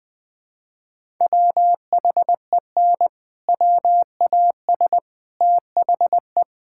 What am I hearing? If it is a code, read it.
Morse code, heard as a keyed tone, about 20 wpm, 700 Hz. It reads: WHEN WAS THE
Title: Sentences with 7-Word Phrases - Spoken Beginning - 20wpm